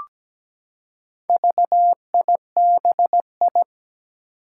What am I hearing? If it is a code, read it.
Morse code, heard as a keyed tone, about 17 wpm, 700 Hz. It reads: VIBI
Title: Four Challenging Letter-Number Combinations 17wpm